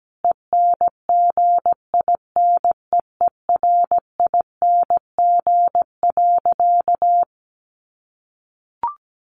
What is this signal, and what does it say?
Morse code, heard as a keyed tone, about 17 wpm, 700 Hz. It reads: ENGINEERING.
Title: Sentences from Top 1500 Words 17wpm